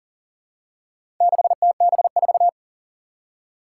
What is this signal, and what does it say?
Morse code, heard as a keyed tone, about 40 wpm, 700 Hz. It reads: 6TB4
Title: Four Challenging Letter-Number Combinations 40wpm